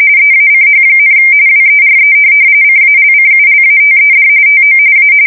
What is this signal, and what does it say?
RTTY, about 45.45 baud. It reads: EACH, WERE CALCULATED TO A